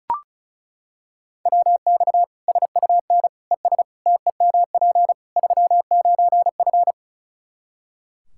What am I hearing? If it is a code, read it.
Morse code, heard as a keyed tone, about 34 wpm, 700 Hz. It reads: WX SUN ES TEMP 39F
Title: Practice QSO - Exchange 2 - Info - 35wpm